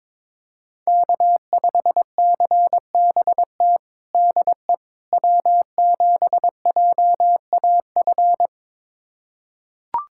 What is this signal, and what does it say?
Morse code, heard as a keyed tone, about 22 wpm, 700 Hz. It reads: K5CBT DE W7JAF
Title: USA Callsign DE Callsign - 5 Characters - 22wpm